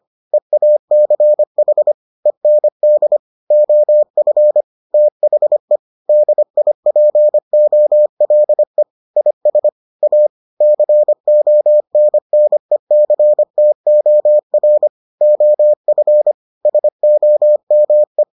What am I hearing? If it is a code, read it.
Morse code, heard as a keyed tone, about 25 wpm, 600 Hz. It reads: EACH END OF THE DIPOLE IS A CONNECTOR OF SOME